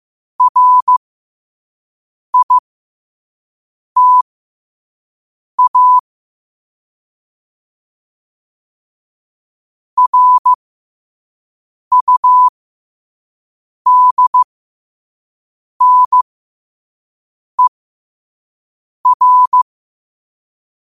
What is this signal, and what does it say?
Morse code, heard as a keyed tone, about 15 wpm, spaced as for 5 wpm, 1000 Hz. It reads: RITA RUDNER
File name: qotdfiles / 5wpm